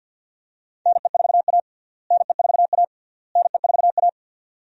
Morse code, heard as a keyed tone, about 52 wpm, 700 Hz. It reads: NE4A NE4A NE4A
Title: USA Call Signs - 4 Characters - Repeated 3x - 50wpm